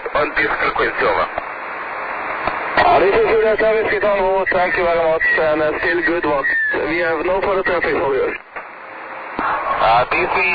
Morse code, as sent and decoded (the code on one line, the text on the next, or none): none